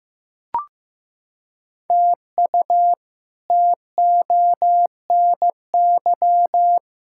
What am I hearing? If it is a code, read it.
Morse code, heard as a keyed tone, about 16 wpm, 700 Hz. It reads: TU TONY
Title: K1USN SST QSO Practice - 15wpm